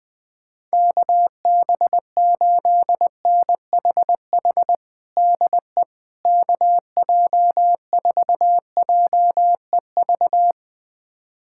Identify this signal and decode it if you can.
Morse code, heard as a keyed tone, about 20 wpm, 700 Hz. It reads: KB8NHH DE KJ4JEV
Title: USA Callsign DE Callsign 20wpm